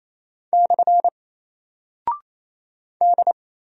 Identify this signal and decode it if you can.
Morse code, heard as a keyed tone, about 27 wpm, 700 Hz. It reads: / D
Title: Single Character (Letter:Number:Punctuation:Prosign) 28wpm